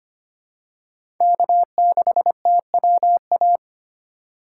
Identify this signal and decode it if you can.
Morse code, heard as a keyed tone, about 25 wpm, 700 Hz. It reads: K6TWA
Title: USA Call Signs - 5 Characters - 2x Character Spacing - 25wpm